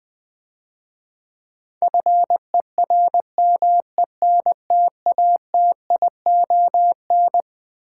Morse code, heard as a keyed tone, about 20 wpm, 700 Hz. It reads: FERMENTATION
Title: Difficult Words - Speed-Racing with Hints - 20wpm